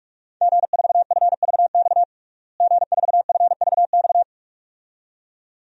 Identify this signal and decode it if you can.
Morse code, heard as a keyed tone, about 46 wpm, 700 Hz. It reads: G4FVX G4FVX
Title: DXCC-100 Call Signs - 5 Characters - Repeated 2x - 45wpm